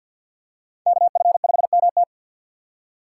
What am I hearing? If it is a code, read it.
Morse code, heard as a keyed tone, about 50 wpm, 700 Hz. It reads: KF5MT
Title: USA Call Signs - 5 Characters - 2x Character Spacing - 50wpm